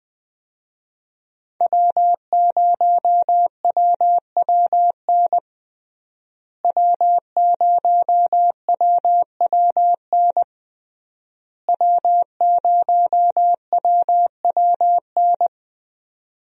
Morse code, heard as a keyed tone, about 20 wpm, 700 Hz. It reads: W0WWN W0WWN W0WWN
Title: USA Call Signs - 5 Characters - Repeated 3x - 20wpm